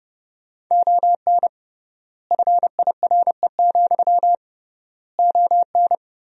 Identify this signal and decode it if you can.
Morse code, heard as a keyed tone, about 30 wpm, 700 Hz. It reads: ON FIRE, ON
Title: 3x Word-Spacing - Sentences from Top 400 Words - 30wpm